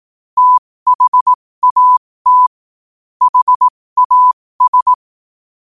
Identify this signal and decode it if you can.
Morse code, heard as a keyed tone, about 18 wpm, spaced as for 15 wpm, 1000 Hz. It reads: THAT HAS